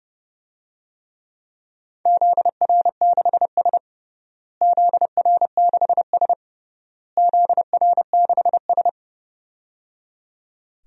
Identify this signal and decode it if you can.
Morse code, heard as a keyed tone, about 30 wpm, 700 Hz. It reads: ZR6S ZR6S ZR6S
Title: DXCC-100 Call Signs - 4 Characters - Repeated 3x - 30wpm